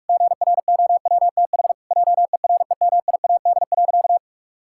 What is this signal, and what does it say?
Morse code, heard as a keyed tone, about 46 wpm, 700 Hz. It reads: GROWTH JEREMIAD.
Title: Pangram Challenge v2 - 45wpm